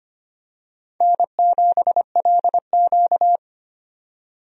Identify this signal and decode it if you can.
Morse code, heard as a keyed tone, about 25 wpm, 700 Hz. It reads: N7LQ
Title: USA Call Signs - 4 Characters - 25wpm